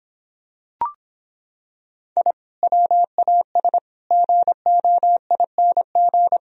Morse code, heard as a keyed tone, about 26 wpm, 700 Hz. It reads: I WAS GOING